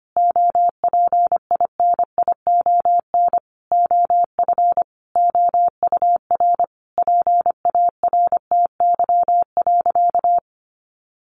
Morse code, heard as a keyed tone, about 25 wpm, 700 Hz. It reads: OPINION OF OUR PARTY.
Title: Sentences from Top 1400 Words 25wpm